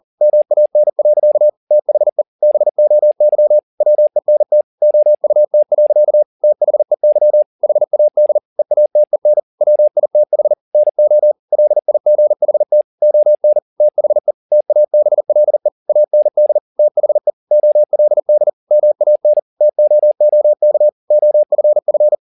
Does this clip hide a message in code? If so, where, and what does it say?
Yes: Morse code from 0.2 to 22.1 s, heard as a keyed tone, about 40 wpm, 600 Hz. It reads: MAN. THE BOY WENT OUT. THEY HAD EATEN WITH NO LIGHT ON THE TABLE AND THE OLD MAN TOOK OFF